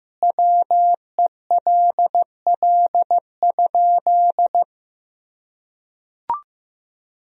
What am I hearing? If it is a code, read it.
Morse code, heard as a keyed tone, about 16 wpm, 700 Hz. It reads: WELL?
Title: Sentences from Top 1000 Words - Review Part 2 - 15wpm